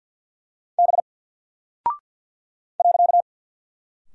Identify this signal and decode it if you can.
Morse code, heard as a keyed tone, about 49 wpm, 700 Hz. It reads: B .
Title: Single Character (Letter:Number:Punctuation:Prosign) 50wpm